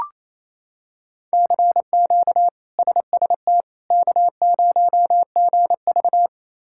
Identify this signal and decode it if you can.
Morse code, heard as a keyed tone, about 28 wpm, 700 Hz. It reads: CQ SST K0GV
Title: K1USN SST QSO Practice - 28wpm